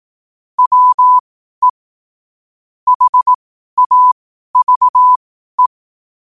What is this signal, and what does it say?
Morse code, heard as a keyed tone, about 18 wpm, spaced as for 12 wpm, 1000 Hz. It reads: WE HAVE
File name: qotdfiles / 12wpm